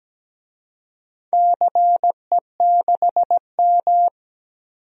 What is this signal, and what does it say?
Morse code, heard as a keyed tone, about 17 wpm, 700 Hz. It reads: CE6M
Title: DXCC-100 Call Signs - 4 Characters - 17wpm